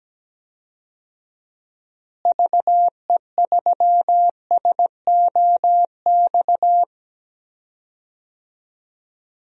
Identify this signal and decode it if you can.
Morse code, heard as a keyed tone, about 17 wpm, 700 Hz. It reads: VE3SOX
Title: DXCC-100 Call Signs - 6 Characters - Repeated 3x - 17wpm